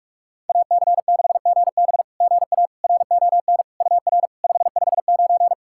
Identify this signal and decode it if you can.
Morse code, heard as a keyed tone, about 44 wpm, 690 Hz. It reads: AC6GB GA RON UR 559
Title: SOTA QSO Practice 45wpm